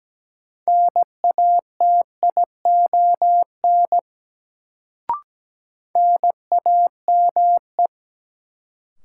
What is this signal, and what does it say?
Morse code, heard as a keyed tone, about 17 wpm, 700 Hz. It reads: NATION NAME
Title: Sets of 3 Words - New Word Drill - 17wpm